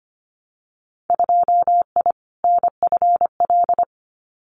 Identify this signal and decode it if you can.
Morse code, heard as a keyed tone, about 25 wpm, 700 Hz. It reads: 2I NFL
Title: Winter Field Day Exchanges 25wpm